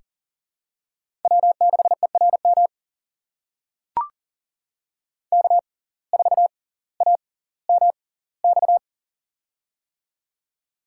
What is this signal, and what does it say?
Morse code, heard as a keyed tone, about 40 wpm, spaced as for 20 wpm, 700 Hz. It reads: W6ERM K4AMX
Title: USA Call Signs - 5 Characters - 6x Character Spacing - 40wpm